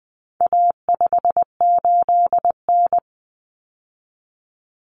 Morse code, heard as a keyed tone, about 20 wpm, 700 Hz. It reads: A58N
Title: Common Cut Numbers - 4-Digit 20wpm